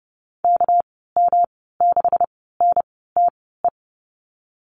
Morse code, heard as a keyed tone, about 30 wpm, spaced as for 17 wpm, 700 Hz. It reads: KM6NTE
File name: Lesson 37g - Letter x - Call Signs 30wpm